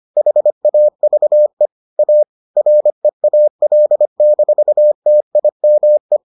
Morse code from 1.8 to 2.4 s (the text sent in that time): A